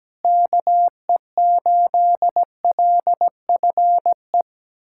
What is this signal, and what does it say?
Morse code, heard as a keyed tone, about 17 wpm, 700 Hz. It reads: KE8LFE
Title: USA Callsign DE Callsign - 6 Characters - 17wpm